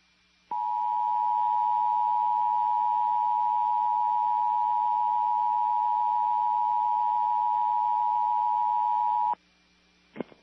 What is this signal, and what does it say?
Emergency Alert System (SAME) tones (the 853 and 960 Hz attention tone). It is heard from 0.5 to 9.4 s.